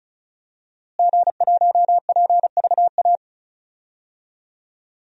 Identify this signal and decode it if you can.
Morse code, heard as a keyed tone, about 36 wpm, 700 Hz. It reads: G1PVA
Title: DXCC-100 Call Signs - 5 Characters - Repeated 2x - 35wpm